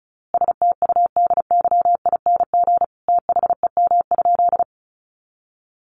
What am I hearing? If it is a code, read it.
Morse code, heard as a keyed tone, about 34 wpm, 700 Hz. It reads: STUDYING THEM?
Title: A Sound Of Thunder 35wpm